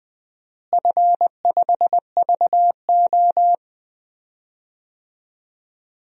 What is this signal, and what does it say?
Morse code, heard as a keyed tone, about 20 wpm, 700 Hz. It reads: F5VO
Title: DXCC-100 Call Signs - 4 Characters - Repeated 3x - 20wpm